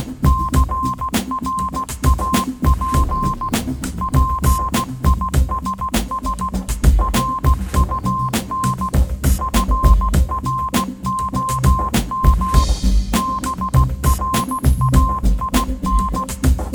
Morse code, heard as a keyed tone, about 16 wpm, 1050 Hz. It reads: CRAP PISS RUN FROM BRAIN
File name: Kirkoid / Kirkoid - Small Stuff (Morse Mix)